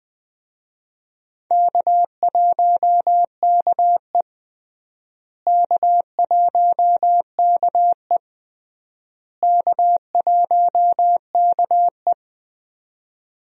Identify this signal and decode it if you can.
Morse code, heard as a keyed tone, about 20 wpm, 700 Hz. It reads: K1KE K1KE K1KE
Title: USA Call Signs - 4 Characters - Repeated 3x - 20wpm